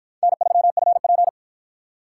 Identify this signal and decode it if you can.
Morse code, heard as a keyed tone, about 47 wpm, 700 Hz. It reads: N3FP